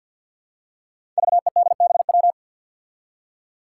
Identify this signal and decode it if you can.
Morse code, heard as a keyed tone, about 49 wpm, 700 Hz. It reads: VEDBW
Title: Five Challenging Letter-Number Combinations 50wpm